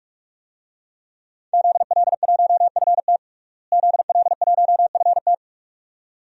Morse code, heard as a keyed tone, about 46 wpm, 700 Hz. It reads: ZL1FT ZL1FT
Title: DXCC-100 Call Signs - 5 Characters - Repeated 2x - 45wpm